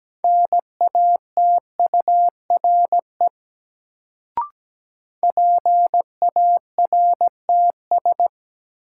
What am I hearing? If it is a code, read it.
Morse code, heard as a keyed tone, about 17 wpm, 700 Hz. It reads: NATURE PARTS
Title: Sets of 3 Words - New Word Drill - 17wpm